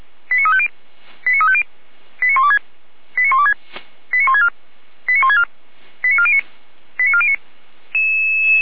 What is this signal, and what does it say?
Five-tone selective call (ZVEI), heard as quick runs of five tones, tones of about 70 ms. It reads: 89350 89350 89126 89126 89164 89164 89409 89409